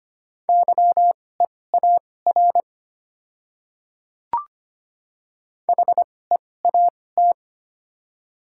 Morse code, heard as a keyed tone, about 25 wpm, spaced as for 18 wpm, 700 Hz. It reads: YEAR HEAT
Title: 2x Character Spacing - Top 500 Words - 25wpm